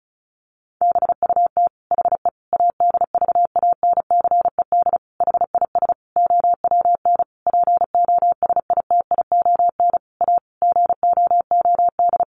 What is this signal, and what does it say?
Morse code, heard as a keyed tone, about 34 wpm, 700 Hz. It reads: BUT HE ADVANCED HIS OWN POSITION A GOOD
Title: Sentences from Top 1200 Words 35wpm